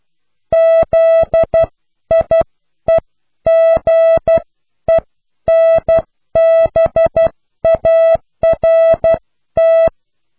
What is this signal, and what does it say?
Morse code, heard as a keyed tone, about 12 wpm, spaced as for 10 wpm, 660 Hz. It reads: ZIEGENBART
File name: cw / pseudo worden(10)